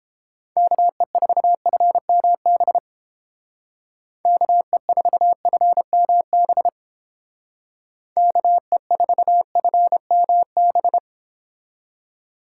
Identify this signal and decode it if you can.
Morse code, heard as a keyed tone, about 28 wpm, 700 Hz. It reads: KE4FMB KE4FMB KE4FMB